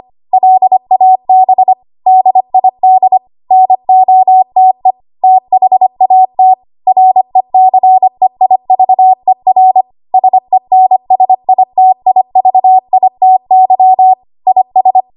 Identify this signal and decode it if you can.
Morse code, heard as a keyed tone, about 25 wpm, 750 Hz. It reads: LAB DID NOTE THAT RECEIVER SENSITIVITY IS
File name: Archive / 25 WPM / 25WPM